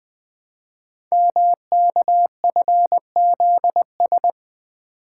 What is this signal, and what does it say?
Morse code, heard as a keyed tone, about 20 wpm, 700 Hz. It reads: MKFZS
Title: Code Groups - 5 Letters - 20wpm